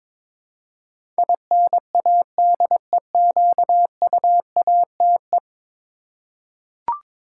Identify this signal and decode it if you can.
Morse code, heard as a keyed tone, about 22 wpm, 700 Hz. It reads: INADEQUATE
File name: practice / 22wpm / 10 Letter Words - First 3 Letters Given - 22wpm